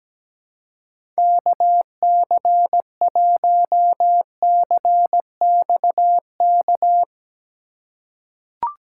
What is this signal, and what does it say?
Morse code, heard as a keyed tone, about 17 wpm, 700 Hz. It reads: KC1CXK
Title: USA Call Signs - 6 Characters - 4x Character Spacing - 17wpm